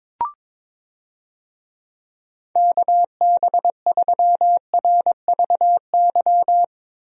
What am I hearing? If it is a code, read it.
Morse code, heard as a keyed tone, about 21 wpm, 700 Hz. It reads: KB3RVY
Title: USA Call Signs - 6 Characters - Repeated 3x - 22wpm